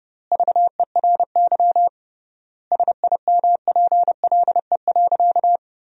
Morse code, heard as a keyed tone, about 30 wpm, 700 Hz. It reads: VERY SIMPLE.
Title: 3x Word-Spacing - Sentences from Top 400 Words - 30wpm